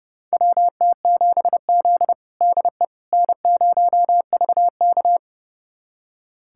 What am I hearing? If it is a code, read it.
Morse code, heard as a keyed tone, about 30 wpm, 700 Hz. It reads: WT7Z DE N0VK
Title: USA Callsign DE Callsign - 4 Characters - 30wpm